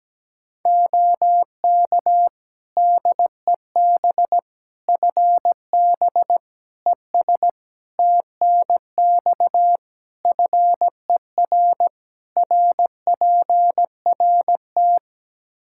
Morse code, heard as a keyed tone, about 17 wpm, 700 Hz. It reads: OK DEB FB ES TNX FER RPRT
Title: Practice QSO - Exchange 2 - Info - 17wpm